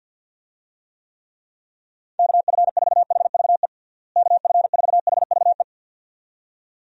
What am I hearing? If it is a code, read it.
Morse code, heard as a keyed tone, about 52 wpm, 700 Hz. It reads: KF4HVE KF4HVE
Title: USA Call Signs - 6 Characters - Repeated 2x - 50wpm